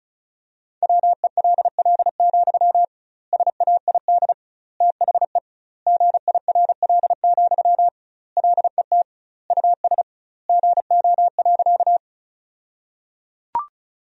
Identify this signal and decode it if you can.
Morse code, heard as a keyed tone, about 36 wpm, 700 Hz. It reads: WELL, SAID THE GIRL, LET US GO.
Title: 2x Word-Spacing - Sentences from Top 400 Words - 35wpm